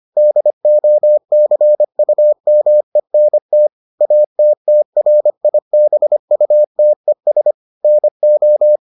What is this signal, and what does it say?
Morse code, heard as a keyed tone, about 25 wpm, 600 Hz. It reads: DOCUMENT ATTRIBUTES NO